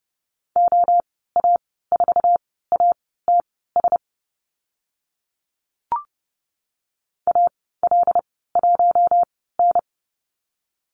Morse code, heard as a keyed tone, about 30 wpm, spaced as for 17 wpm, 700 Hz. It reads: OA4ATS AL1N
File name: Lesson 15g - Number 5 - Call Signs 30wpm